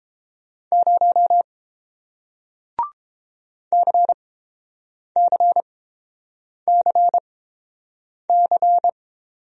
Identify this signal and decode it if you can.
Morse code, heard as a keyed tone, about 31 wpm, spaced as for 7 wpm, 700 Hz. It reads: 0 CCCC